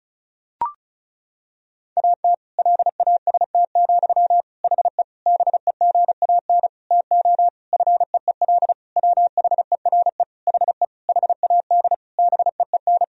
Morse code, heard as a keyed tone, about 34 wpm, 700 Hz. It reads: AT LAST, HE BEGAN TO FEEL WHERE HE HAD BEEN
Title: Sentences from Top 1000 Words - Review Part 1 - 35wpm